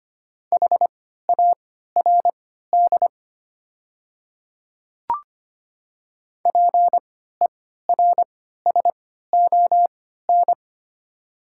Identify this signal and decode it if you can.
Morse code, heard as a keyed tone, about 25 wpm, spaced as for 14 wpm, 700 Hz. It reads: HARD PERSON
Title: 3x Character Spacing - Top 500 Words - 25wpm